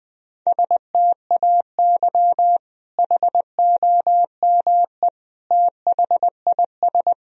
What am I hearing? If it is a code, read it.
Morse code, heard as a keyed tone, about 20 wpm, 700 Hz. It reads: STAY HOME THIS
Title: Sentences from Top 1000 Words - Review Part 1 - 20wpm